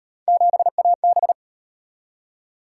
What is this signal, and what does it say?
Morse code, heard as a keyed tone, about 38 wpm, 700 Hz. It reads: 7AB